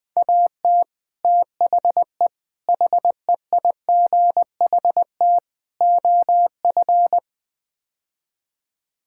Morse code, heard as a keyed tone, about 20 wpm, 700 Hz. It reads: AT THE HEIGHT OF
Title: Build A Phrase - 5-Word Phrases - 20wpm